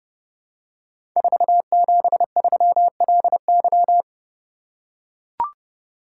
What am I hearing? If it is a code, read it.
Morse code, heard as a keyed tone, about 30 wpm, 700 Hz. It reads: 473LY
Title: Code Groups - 5 Letter:Numbers - 30wpm